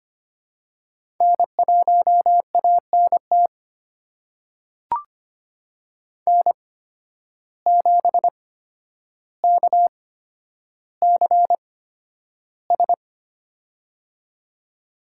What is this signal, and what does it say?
Morse code, heard as a keyed tone, about 25 wpm, spaced as for 11 wpm, 700 Hz. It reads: N1ANT N7KCS
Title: USA Call Signs - 5 Characters - 8x Character Spacing - 25wpm